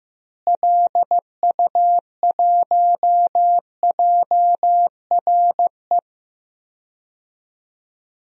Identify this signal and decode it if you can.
Morse code, heard as a keyed tone, about 15 wpm, 700 Hz. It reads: LU1JRE
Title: DXCC-100 Call Signs - 6 Characters - Repeated 3x - 15wpm